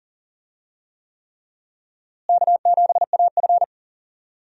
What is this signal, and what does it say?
Morse code, heard as a keyed tone, about 38 wpm, 700 Hz. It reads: K7AF
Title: USA Call Signs - 4 Characters - Repeated 3x - 40wpm